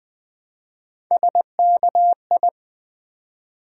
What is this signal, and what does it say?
Morse code, heard as a keyed tone, about 20 wpm, 700 Hz. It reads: SKI
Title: Build A Word - 1 Letter At A Time - 5-Letter Words - 20wpm